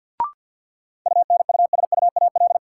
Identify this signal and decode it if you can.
Morse code, heard as a keyed tone, about 53 wpm, 700 Hz. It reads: UNUSUAL
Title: Sentences from Top 1400 Words - New Word Drill Alt - 50wpm